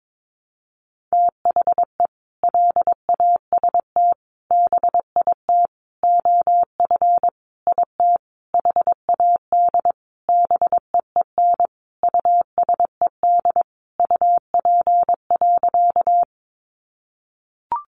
Morse code, heard as a keyed tone, about 22 wpm, 700 Hz. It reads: THE LAST BIT OF IT HAD BEEN USED UP.
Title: Sentences from Top 900 Words 22wpm